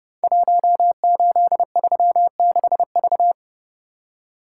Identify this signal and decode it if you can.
Morse code, heard as a keyed tone, about 30 wpm, 700 Hz. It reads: 1836V
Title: Code Groups - 5 Letter:Numbers - 30wpm